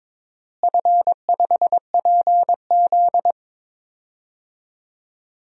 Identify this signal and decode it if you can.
Morse code, heard as a keyed tone, about 22 wpm, 700 Hz. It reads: F5PZ
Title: DXCC-100 Call Signs - 4 Characters - Repeated 3x - 22wpm